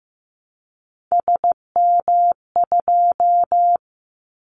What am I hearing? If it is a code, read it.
Morse code, heard as a keyed tone, about 15 wpm, 700 Hz. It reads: SM2
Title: Callsign Prefixes for Top 25 DXCC - Repeated 3x - 15wpm